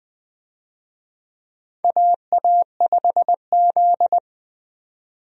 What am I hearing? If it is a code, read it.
Morse code, heard as a keyed tone, about 20 wpm, 700 Hz. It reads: AA5Z
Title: USA Call Signs - 4 Characters - 20wpm